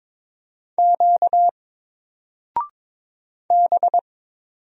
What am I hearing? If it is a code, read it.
Morse code, heard as a keyed tone, about 22 wpm, 700 Hz. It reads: Q B
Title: Single Character (Letter:Number:Punctuation:Prosign) Alt 22wpm